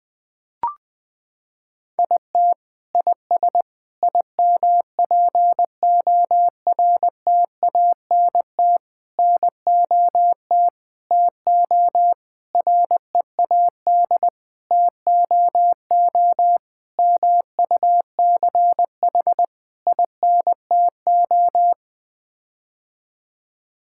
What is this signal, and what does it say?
Morse code, heard as a keyed tone, about 20 wpm, 700 Hz. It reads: IT IS IMPORTANT NOT TO READ TOO MUCH INTO
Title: Sets of 9 Words - 20wpm